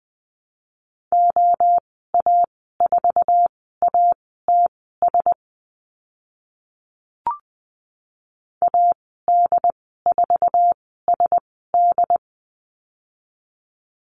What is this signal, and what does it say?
Morse code, heard as a keyed tone, about 20 wpm, spaced as for 14 wpm, 700 Hz. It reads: OA4ATS AD4SD
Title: 076 Lesson 13g - Letter L - Call Signs 20wpm